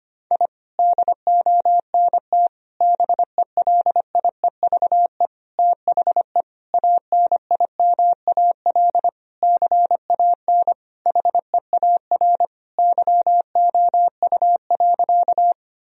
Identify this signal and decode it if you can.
Morse code, heard as a keyed tone, about 25 wpm, 700 Hz. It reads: I DONT BELIEVE THE ANIMAL CAN HEAR YOU.
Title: Sentences from Top 1000 Words - Review Part 1 - 25wpm